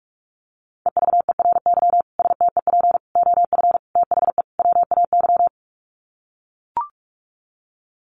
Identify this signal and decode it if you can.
Morse code, heard as a keyed tone, about 44 wpm, 700 Hz. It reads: EVERY STEP OF THE WAY
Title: Sets of 5 Words Encore2 45wpm